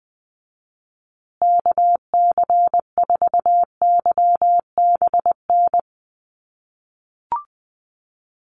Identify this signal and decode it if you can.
Morse code, heard as a keyed tone, about 20 wpm, 700 Hz. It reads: KC4YBN
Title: USA Call Signs Repeated 3x 20wpm